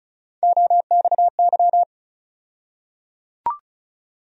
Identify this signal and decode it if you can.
Morse code, heard as a keyed tone, about 34 wpm, 700 Hz. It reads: OXY
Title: USA Call Sign Postfixes - 3 Letters 35wpm